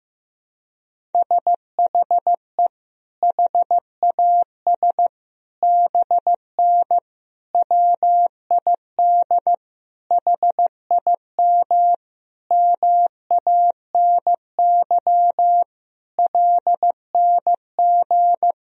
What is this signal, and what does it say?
Morse code, heard as a keyed tone, about 15 wpm, 700 Hz. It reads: SHE HAS BN WID HIM MANY LNG
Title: Sentences from Top 200 Words and Abbreviations 15wpm